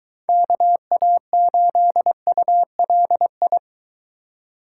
Morse code, heard as a keyed tone, about 22 wpm, 700 Hz. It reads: KA8ULI